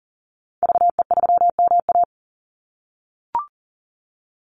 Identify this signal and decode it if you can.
Morse code, heard as a keyed tone, about 40 wpm, 700 Hz. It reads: VE3MA
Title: Call Signs - 40wpm